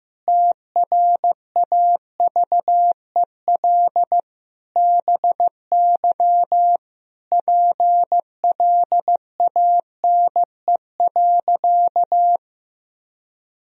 Morse code, heard as a keyed tone, about 15 wpm, 700 Hz. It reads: TRAVEL BY PLANE.
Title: Sentences from Top 1000 Words - Review Part 3 - 15wpm